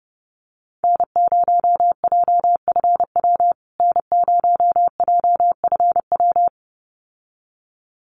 Morse code, heard as a keyed tone, about 30 wpm, 700 Hz. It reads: N0JFW N0JFW
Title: USA Call Signs Repeated 2x 30wpm